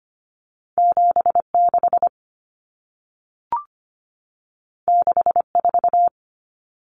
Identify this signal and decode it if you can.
Morse code, heard as a keyed tone, about 25 wpm, 700 Hz. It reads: 76 64
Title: Numbers - 2-Digit 25wpm